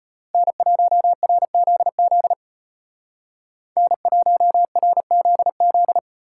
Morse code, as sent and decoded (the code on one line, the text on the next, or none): -. .---- .-. --.. --.. / -. .---- .-. --.. --..
N1RZZ N1RZZ